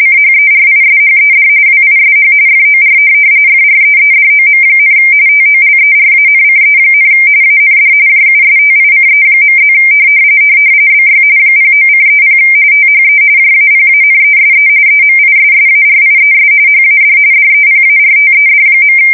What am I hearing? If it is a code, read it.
RTTY, about 45.45 baud. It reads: CHERISHED WISH FOR FREEDOM, TOLERANCE, AND JUSTICE.+ BY 09:30 THE JAPANESE EMISSARIES HAD DEPARTED.
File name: fsk170 / USS-Missouri